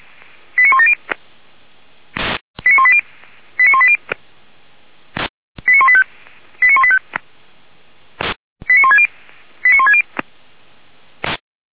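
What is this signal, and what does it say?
Five-tone selective call (ZVEI), heard as quick runs of five tones, tones of about 80 ms. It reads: 89170 89180 89180 89185 89185 89160 89160
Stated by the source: ZVEI five-tone call received by SDR, archived by feuerwehr-orlamuende.org fire brigade server.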